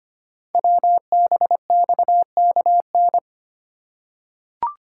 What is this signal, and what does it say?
Morse code, heard as a keyed tone, about 25 wpm, 700 Hz. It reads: WBXKN
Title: Code Groups - 5 Letters - 25wpm